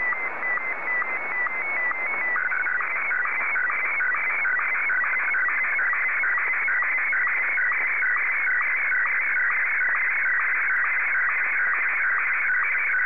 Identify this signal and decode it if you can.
SSTV (slow-scan television), Martin 1.
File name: sstv / M1-VK3